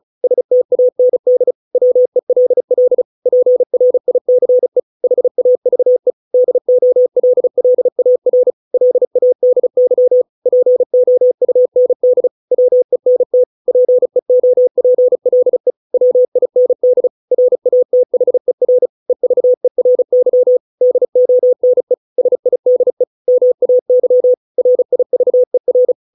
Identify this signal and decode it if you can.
Morse code, heard as a keyed tone, about 34 wpm, 500 Hz. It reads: STAND WELL PRICE HAVE DOLLAR LADY POUND WENT PEOPLE WIND RATHER EVERY DONE SIDE MAY RIVER